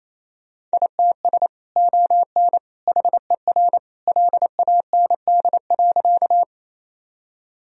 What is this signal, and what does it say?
Morse code, heard as a keyed tone, about 28 wpm, 700 Hz. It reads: ITS ON HER LAND.
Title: Sentences from Top 1000 Words - Review Part 1 - 28wpm